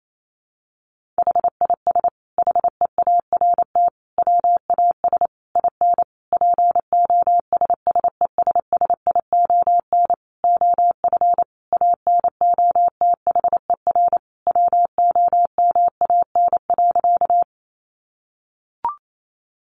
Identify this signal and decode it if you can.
Morse code, heard as a keyed tone, about 28 wpm, 700 Hz. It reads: HIS HEART WAS IN POSSESSION OF ANOTHER WOMAN.
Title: Sentences from Top 1400 Words 28wpm